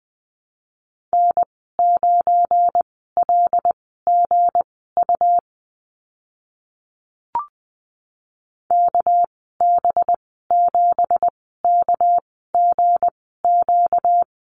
Morse code, heard as a keyed tone, about 20 wpm, spaced as for 14 wpm, 700 Hz. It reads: N9LGU KB7KGQ
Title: 245 Lesson 39g - Letter z - Call Signs 20wpm